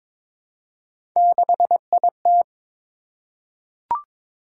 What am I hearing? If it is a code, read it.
Morse code, heard as a keyed tone, about 22 wpm, 700 Hz. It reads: 6IT
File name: practice / 22wpm / Three Letter-Number Combinations - 22wpm